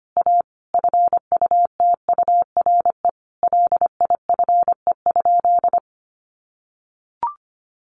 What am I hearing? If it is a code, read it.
Morse code, heard as a keyed tone, about 25 wpm, 700 Hz. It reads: A FUTURE LIFE?
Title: Sentences from Top 1000 Words 25wpm